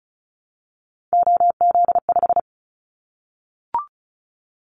Morse code, heard as a keyed tone, about 34 wpm, 700 Hz. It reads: OZ5
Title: Callsign Prefixes for Top 100 DXCC - 35wpm